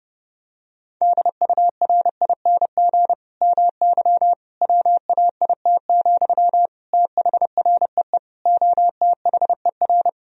Morse code, heard as a keyed tone, about 32 wpm, 700 Hz. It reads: DURING MY WAIT, THREE OTHER
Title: Sentences from Top 1000 Words - Review Part 2 - 30wpm